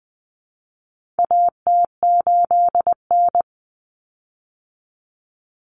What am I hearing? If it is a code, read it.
Morse code, heard as a keyed tone, about 20 wpm, 700 Hz. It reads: AT8N
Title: Common Cut Numbers - 4-Digit 20wpm